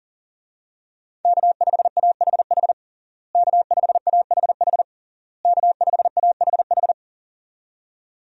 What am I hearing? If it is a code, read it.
Morse code, heard as a keyed tone, about 37 wpm, 690 Hz. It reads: K5AHH K5AHH K5AHH
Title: USA Call Signs - 5 Characters - Repeated 3x - 40wpm